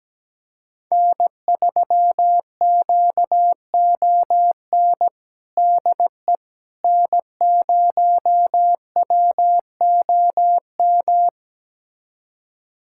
Morse code, heard as a keyed tone, about 17 wpm, 700 Hz. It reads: N3QON DE N0WOM
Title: USA Callsign DE Callsign - 5 Characters - 17wpm